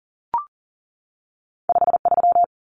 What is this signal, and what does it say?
Morse code, heard as a keyed tone, about 40 wpm, 700 Hz. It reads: 53
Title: Common Cut Numbers - 2-Digit 40wpm